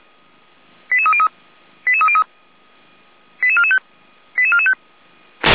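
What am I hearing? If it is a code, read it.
Five-tone selective call (ZVEI), heard as quick runs of five tones, tones of about 70 ms. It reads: 88303 88303 88445 88445
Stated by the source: ZVEI five-tone selective call received by SDR, address 88261.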